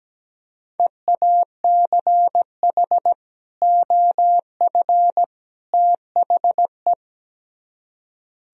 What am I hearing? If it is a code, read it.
Morse code, heard as a keyed tone, about 19 wpm, 700 Hz. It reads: EACH OF THE
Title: Speed-Racing - Sets of 3 Words 17wpm